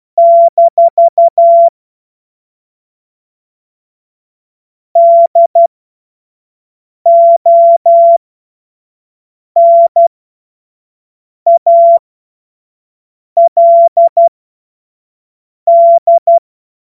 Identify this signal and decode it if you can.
Morse code, heard as a keyed tone, about 12 wpm, spaced as for 5 wpm, 680 Hz. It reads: - DONALD
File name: Morsekurs / Witze / Witz 07. Juni 2022 25 F70